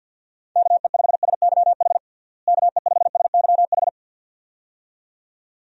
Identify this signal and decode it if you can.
Morse code, heard as a keyed tone, about 52 wpm, 700 Hz. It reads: KE5SYH KE5SYH
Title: USA Call Signs - 6 Characters - Repeated 2x - 50wpm